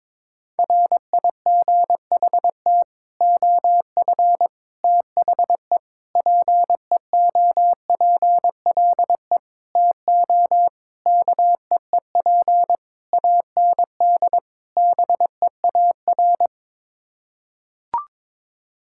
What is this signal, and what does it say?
Morse code, heard as a keyed tone, about 22 wpm, 700 Hz. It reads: RIGHT OF THE PEOPLE TO KEEP AND BEAR
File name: practice / 22wpm / Sets of 8 Words - 22wpm